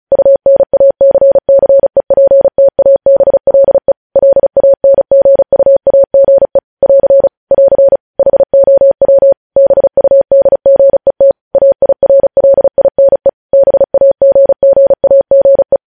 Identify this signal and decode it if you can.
Morse code, heard as a keyed tone, about 36 wpm, 550 Hz. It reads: UNACCEPTABLE LANGUAGE <AR> <AR> HOW BUDGET AIRLINE BAGGAGE
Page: news headlines in morse code at 35 wpm